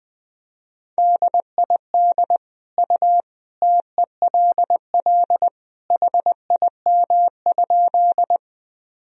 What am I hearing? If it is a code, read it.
Morse code, heard as a keyed tone, about 20 wpm, 700 Hz. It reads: DID U TELL HIM?
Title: Sentences from Top 200 Words and Abbreviations 20wpm